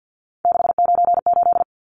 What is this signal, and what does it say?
Morse code, heard as a keyed tone, about 50 wpm, 700 Hz. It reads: 698
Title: Numbers - 3-Digit 50wpm